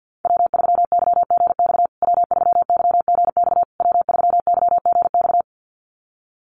Morse code, heard as a keyed tone, about 50 wpm, 700 Hz. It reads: W3YGX W3YGX W3YGX
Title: USA Call Signs Repeated 3x 50wpm